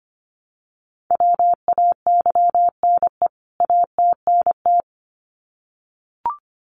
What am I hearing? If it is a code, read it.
Morse code, heard as a keyed tone, about 25 wpm, 700 Hz. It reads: WAYNE ATNT
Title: CWT Exchange 25wpm